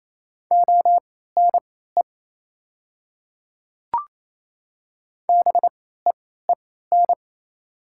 Morse code, heard as a keyed tone, about 28 wpm, spaced as for 16 wpm, 690 Hz. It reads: ONE BEEN
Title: 3x Character Spacing - Top 100 Words - 28wpm